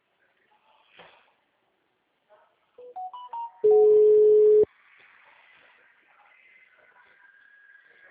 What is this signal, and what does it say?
Ringback tone (Europe, 425 Hz).
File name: out- -2019- - - .369